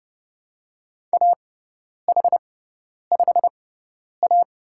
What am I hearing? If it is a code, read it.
Morse code, heard as a keyed tone, about 30 wpm, spaced as for 10 wpm, 700 Hz. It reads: AH5A
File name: Lesson 16g - Letter c - Call Signs 30wpm